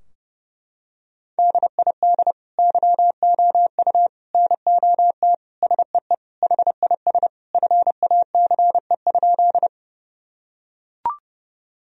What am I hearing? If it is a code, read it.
Morse code, heard as a keyed tone, about 30 wpm, 700 Hz. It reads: DID YOU NOT SEE HIS FACE?
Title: Sentences from Top 1000 Words - Review Part 2 - 30wpm